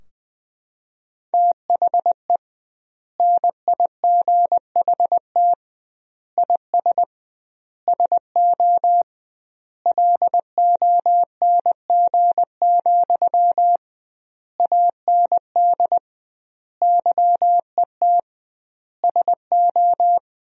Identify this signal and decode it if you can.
Morse code, heard as a keyed tone, about 21 wpm, 700 Hz. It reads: THE NIGHT IS SO LONG, AND YET SO
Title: 2x Word-Spacing - Sentences from Top 500 Words - 20wpm